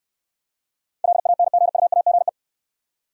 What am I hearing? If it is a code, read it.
Morse code, heard as a keyed tone, about 68 wpm, 700 Hz. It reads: LANGUAGE